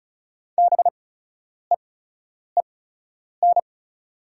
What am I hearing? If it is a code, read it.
Morse code, heard as a keyed tone, about 36 wpm, spaced as for 10 wpm, 700 Hz. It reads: BEEN
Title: 8x Character Spacing - Top 300 Words - 35wpm